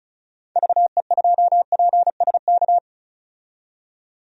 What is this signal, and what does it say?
Morse code, heard as a keyed tone, about 34 wpm, 700 Hz. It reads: VE2PSK
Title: DXCC-100 Call Signs - 6 Characters - 35wpm